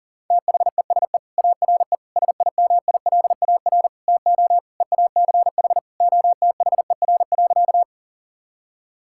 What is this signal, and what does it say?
Morse code, heard as a keyed tone, about 37 wpm, 700 Hz. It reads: THESE ARE SIMILAR TO EACH OTHER.
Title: Sentences from Top 1000 Words - Review Part 5 - 40wpm